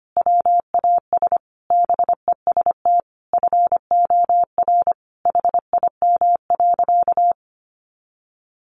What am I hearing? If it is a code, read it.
Morse code, heard as a keyed tone, about 25 wpm, 700 Hz. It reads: WAS BEST FOR HIM.
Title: Sentences from Top 500 Words 25wpm